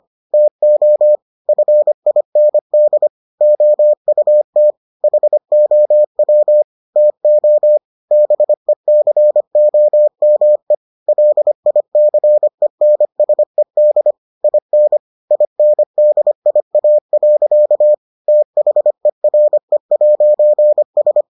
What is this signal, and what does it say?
Morse code, heard as a keyed tone, about 25 wpm, 600 Hz. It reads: TO FIND OUT HOW TO BECOME LICENSED IN INDIA. THERE'S